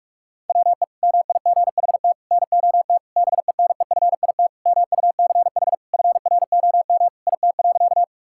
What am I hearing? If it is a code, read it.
Morse code, heard as a keyed tone, about 46 wpm, 700 Hz. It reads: WE MIGHT NOT BENEFIT MUCH FROM IT.
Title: Sentences from Top 1000 Words - Review Part 6 - 45wpm